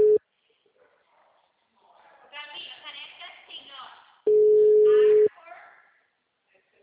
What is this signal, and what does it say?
Ringback tone (Europe, 425 Hz), with speech.